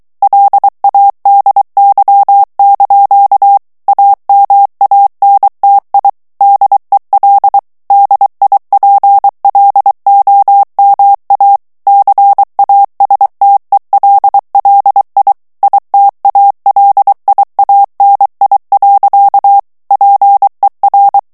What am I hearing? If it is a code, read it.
Morse code, heard as a keyed tone, about 23 wpm, 800 Hz. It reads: LADY) AMANTI DEL DIPLOMA CASTELLI ITALIANI. PER
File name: morse mp3 / ITALIEN texte clair 1 vitesse 22 mots minute